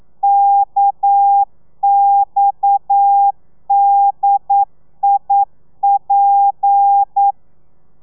Morse code, heard as a keyed tone, about 9 wpm, 800 Hz. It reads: KXDIP
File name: telegrafia / i2tfj / Lez7 40